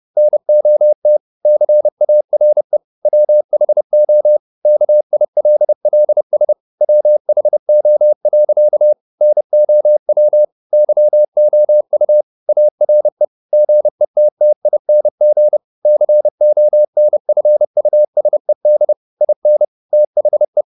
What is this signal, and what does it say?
Morse code, heard as a keyed tone, about 30 wpm, 600 Hz. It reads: NOT CARE WHO KILLS WHO. NOW YOU ARE GETTING CONFUSED IN THE